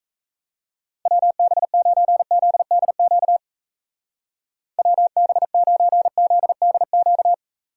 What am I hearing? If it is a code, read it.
Morse code, heard as a keyed tone, about 34 wpm, 700 Hz. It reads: WB9ZDQ WB9ZDQ
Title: Speed Racing - USA Call Signs 28wpm